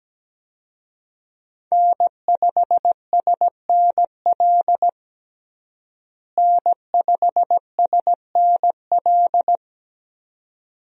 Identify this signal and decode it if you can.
Morse code, heard as a keyed tone, about 17 wpm, 700 Hz. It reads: N5SNL N5SNL
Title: USA Call Signs - 5 Characters - Repeated 2x - 17wpm